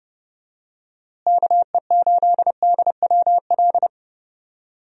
Morse code, heard as a keyed tone, about 32 wpm, 700 Hz. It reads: KE8DWL
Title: USA Call Signs - 6 Characters - Repeated 3x - 30wpm